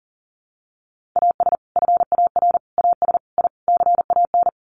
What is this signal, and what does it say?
Morse code, heard as a keyed tone, about 40 wpm, 700 Hz. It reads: AS FAR AS I CAN
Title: Sets of 5 Words Encore 40wpm